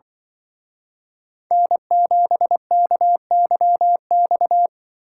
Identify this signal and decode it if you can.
Morse code, heard as a keyed tone, about 24 wpm, 700 Hz. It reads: N7KYX
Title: Speed Racing - USA Call Signs 20wpm